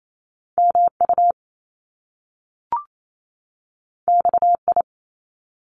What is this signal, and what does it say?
Morse code, heard as a keyed tone, about 28 wpm, 700 Hz. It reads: MU XI
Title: Top Two Letter Combinations in Words 28wpm